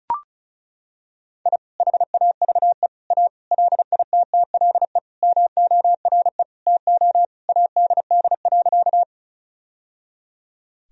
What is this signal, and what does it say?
Morse code, heard as a keyed tone, about 36 wpm, 700 Hz. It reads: I HAVE A LITTLE MORE TO ADD.
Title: Sentences from Top 1000 Words - Review Part 1 - 35wpm